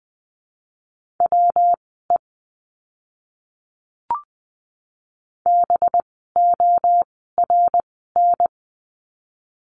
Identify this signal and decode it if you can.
Morse code, heard as a keyed tone, about 20 wpm, spaced as for 14 wpm, 700 Hz. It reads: WE BORN
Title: 185 Lesson 30f - Letter b - Words 20wpm